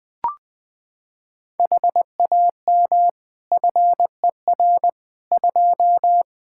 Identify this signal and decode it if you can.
Morse code, heard as a keyed tone, about 20 wpm, 700 Hz. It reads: HAM FER 2
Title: Practice QSO - Exchange 3 - More Info - 20wpm